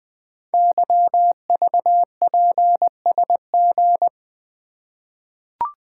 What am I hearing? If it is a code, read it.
Morse code, heard as a keyed tone, about 20 wpm, 700 Hz. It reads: YVPSG
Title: Code Groups - 5 Letters - 20wpm